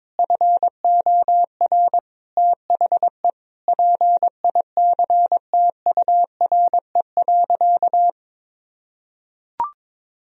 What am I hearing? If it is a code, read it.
Morse code, heard as a keyed tone, about 22 wpm, 700 Hz. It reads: FOR THE PICTURE.
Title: Sentences from Top 1000 Words - Review Part 2 - 22wpm